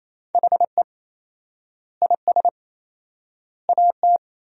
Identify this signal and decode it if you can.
Morse code, heard as a keyed tone, about 28 wpm, 700 Hz. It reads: HE IS AT
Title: 4x Word-Spacing - Sentences from Top 200 Words - 28wpm